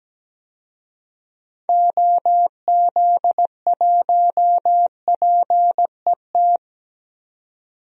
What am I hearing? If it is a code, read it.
Morse code, heard as a keyed tone, about 17 wpm, 700 Hz. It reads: OZ1PET
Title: DXCC-100 Call Signs - 6 Characters - 17wpm